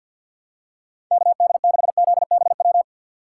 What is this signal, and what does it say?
Morse code, heard as a keyed tone, about 46 wpm, 700 Hz. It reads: KD6ZBW
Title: USA Call Signs - 6 Characters - 6x Character Spacing - 50wpm